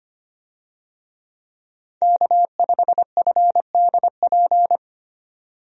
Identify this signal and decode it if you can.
Morse code, heard as a keyed tone, about 25 wpm, 700 Hz. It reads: K5FDP
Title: USA Call Signs - 5 Characters - 25wpm